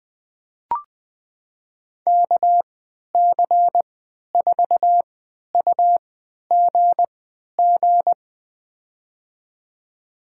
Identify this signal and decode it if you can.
Morse code, heard as a keyed tone, about 20 wpm, spaced as for 11 wpm, 700 Hz. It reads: KC4UGG
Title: USA Call Signs - 6 Characters - 3x Character Spacing - 20wpm